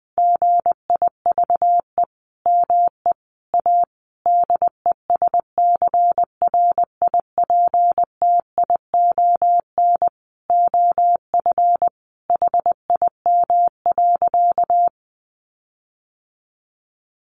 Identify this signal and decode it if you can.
Morse code, heard as a keyed tone, about 20 wpm, 700 Hz. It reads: GIVE ME A DESCRIPTION OF HIM.
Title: Sentences from Top 1300 Words 20wpm